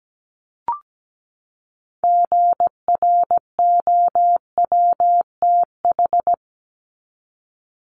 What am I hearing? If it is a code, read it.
Morse code, heard as a keyed tone, about 17 wpm, 700 Hz. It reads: GROWTH
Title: Top 751-1000 Words 17wpm